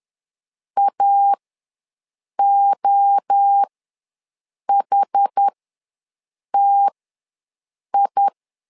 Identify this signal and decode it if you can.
Morse code, heard as a keyed tone, about 10 wpm, spaced as for 6 wpm, 800 Hz. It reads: AOHTI
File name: LEZIONI / 3 MWRO / 3 MWRO ws 10wpm 2310